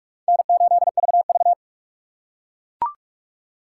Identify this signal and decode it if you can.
Morse code, heard as a keyed tone, about 43 wpm, 700 Hz. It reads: N8VV